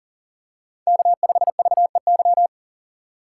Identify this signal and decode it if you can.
Morse code, heard as a keyed tone, about 40 wpm, 700 Hz. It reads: K5VEY
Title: USA Call Signs - 5 Characters - 2x Character Spacing - 40wpm